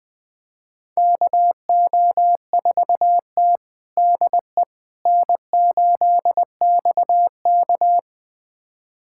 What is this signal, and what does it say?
Morse code, heard as a keyed tone, about 20 wpm, 700 Hz. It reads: KO4T DE N8XK
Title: USA Callsign DE Callsign - 4 Characters - 20wpm